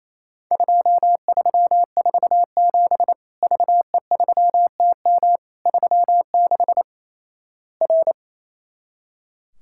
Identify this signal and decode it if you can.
Morse code, heard as a keyed tone, about 28 wpm, 700 Hz. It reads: 2347 VE3TM 36 R
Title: Worked All Europe DX Contest - QTC Exchange - 28wpm